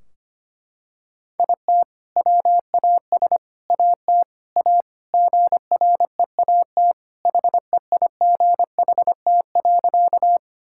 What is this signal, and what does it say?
Morse code, heard as a keyed tone, about 25 wpm, 700 Hz. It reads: IT WAS AT A GREAT HEIGHT.
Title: Sentences from Top 1600 Words 25wpm